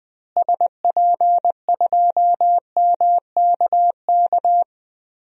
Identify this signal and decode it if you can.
Morse code, heard as a keyed tone, about 20 wpm, 700 Hz. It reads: SP2MKK
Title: DXCC-100 Call Signs - 6 Characters - Repeated 3x - 20wpm